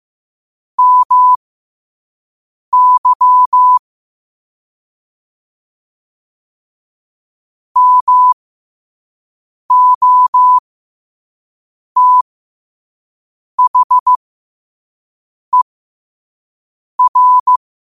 Morse code, heard as a keyed tone, about 15 wpm, spaced as for 5 wpm, 1000 Hz. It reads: MY MOTHER
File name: qotdfiles / 5wpm